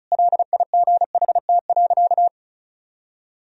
Morse code, heard as a keyed tone, about 34 wpm, 700 Hz. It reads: LIGHT.
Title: Sentences from Top 1000 Words - Review Part 1 - 35wpm